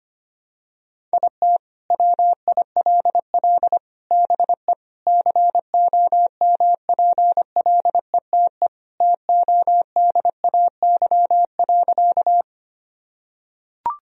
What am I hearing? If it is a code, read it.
Morse code, heard as a keyed tone, about 25 wpm, 700 Hz. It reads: IT WILL BE COMPLETE TODAY.
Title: Sentences from Top 1000 Words - Review Part 2 - 25wpm